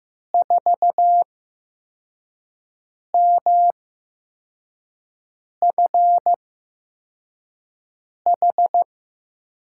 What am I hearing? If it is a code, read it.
Morse code, heard as a keyed tone, about 15 wpm, spaced as for 4 wpm, 700 Hz. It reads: 4MFH
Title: USA Call Signs - 6 Characters - 8x Character Spacing - 15wpm